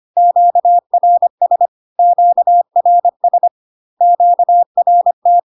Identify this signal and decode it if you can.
Morse code, heard as a keyed tone, about 25 wpm, 700 Hz. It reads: QRS QRS QRT
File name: Q-codes most common repeated 3 times 25 25wpm